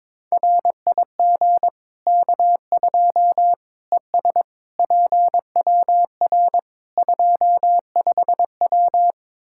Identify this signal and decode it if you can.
Morse code, heard as a keyed tone, about 22 wpm, 700 Hz. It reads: RIG K2 ES PWR 25W
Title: Practice QSO - Exchange 2 - Info - 22wpm